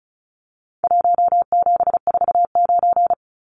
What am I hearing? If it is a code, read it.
Morse code, heard as a keyed tone, about 34 wpm, 700 Hz. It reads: 1749
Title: Numbers - Contest Style - 4-Digit 35wpm